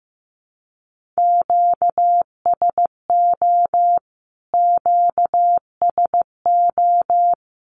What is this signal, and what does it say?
Morse code, heard as a keyed tone, about 15 wpm, 700 Hz. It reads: QSO QSO
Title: Q-Codes Repeated 3x 15wpm